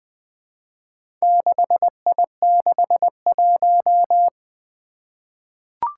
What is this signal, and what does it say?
Morse code, heard as a keyed tone, about 20 wpm, 700 Hz. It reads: 6I61
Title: Four Challenging Letter-Number Combinations 20wpm